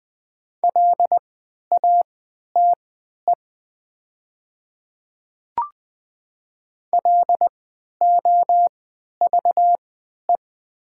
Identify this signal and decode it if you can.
Morse code, heard as a keyed tone, about 20 wpm, spaced as for 11 wpm, 700 Hz. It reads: LATE LOVE
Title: 3x Character Spacing - Top 500 Words - 20wpm